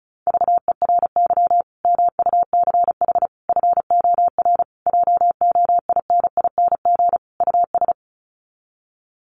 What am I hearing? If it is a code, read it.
Morse code, heard as a keyed tone, about 34 wpm, 700 Hz. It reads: VERY MUCH FOR JOINING US
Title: Sets of 5 Words 35wpm